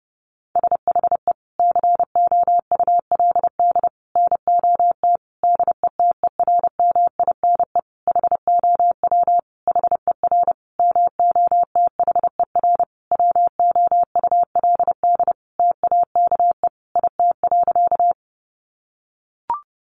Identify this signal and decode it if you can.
Morse code, heard as a keyed tone, about 30 wpm, 700 Hz. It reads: SHE COULD NOT DETERMINE HOW HER MOTHER WOULD TAKE IT.
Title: Sentences from Top 700 Words 30wpm